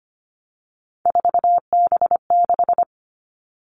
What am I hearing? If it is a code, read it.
Morse code, heard as a keyed tone, about 25 wpm, 700 Hz. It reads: 4B6
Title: All Cut Numbers - 3-Digit 25wpm